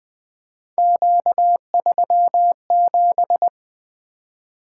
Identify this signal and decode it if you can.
Morse code, heard as a keyed tone, about 20 wpm, 700 Hz. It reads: Q37
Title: Three Letter-Number Combinations - 20wpm